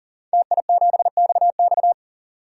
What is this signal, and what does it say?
Morse code, heard as a keyed tone, about 37 wpm, 700 Hz. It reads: TI7XX
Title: DXCC-100 Call Signs - 5 Characters - 40wpm